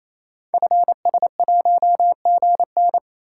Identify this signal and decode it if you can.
Morse code, heard as a keyed tone, about 27 wpm, 700 Hz. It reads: FS1GN
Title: Code Groups - 5 Letter:Numbers - 28wpm